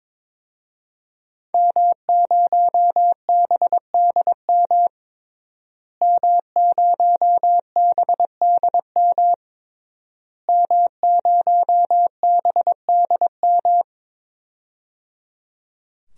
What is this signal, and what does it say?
Morse code, heard as a keyed tone, about 22 wpm, 700 Hz. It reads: M0BDM M0BDM M0BDM
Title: DXCC-100 Call Signs - 5 Characters - Repeated 3x - 22wpm